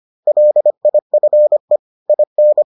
Morse code, heard as a keyed tone, about 25 wpm, 600 Hz. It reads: LIFE IN